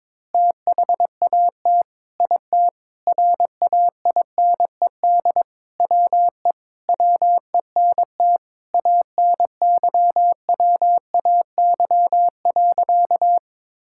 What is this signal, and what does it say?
Morse code, heard as a keyed tone, about 22 wpm, 700 Hz. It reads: THAT IT RAINED WE WENT ANYWAY.
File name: practice / 22wpm / Sentences with 4-Word Phrases - 22wpm